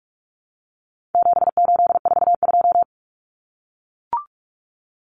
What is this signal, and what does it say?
Morse code, heard as a keyed tone, about 44 wpm, 700 Hz. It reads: 7842
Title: Numbers - 4-Digit 45wpm